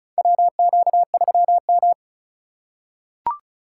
Morse code, heard as a keyed tone, about 36 wpm, 700 Hz. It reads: WQ3M
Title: USA Call Signs - 4 Characters - 2x Character Spacing - 35wpm